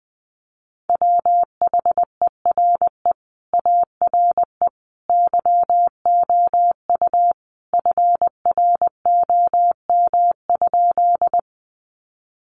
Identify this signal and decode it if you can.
Morse code, heard as a keyed tone, about 20 wpm, 700 Hz. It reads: WHERE ARE YOU FROM?
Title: Sentences from Top 200 Words 20wpm